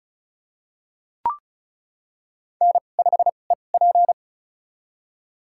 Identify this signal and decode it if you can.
Morse code, heard as a keyed tone, about 34 wpm, spaced as for 25 wpm, 700 Hz. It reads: N5EP
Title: USA Call Signs - 4 Characters - 2x Character Spacing - 35wpm